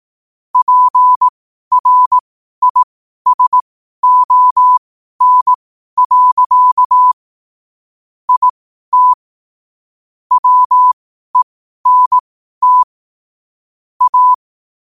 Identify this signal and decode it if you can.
Morse code, heard as a keyed tone, about 18 wpm, spaced as for 12 wpm, 1000 Hz. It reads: PRISON. IT WENT A
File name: qotdfiles / 12wpm